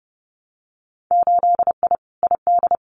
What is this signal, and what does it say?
Morse code, heard as a keyed tone, about 30 wpm, 700 Hz. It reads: 8I ID
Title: Winter Field Day Exchanges Encore 30wpm